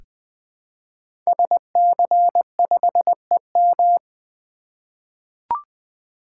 Morse code, heard as a keyed tone, about 20 wpm, 700 Hz. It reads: SC5EM
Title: Code Groups - 5 Letter:Numbers - 20wpm